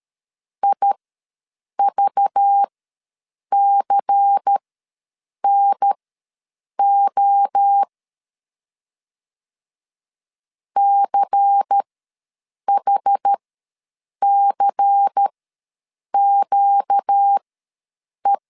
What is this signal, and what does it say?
Morse code, heard as a keyed tone, about 13 wpm, spaced as for 7 wpm, 800 Hz. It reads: IVCNO CHCQE
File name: LEZIONI / 5 LBVC / 5 LBVC ws 12wpm 3159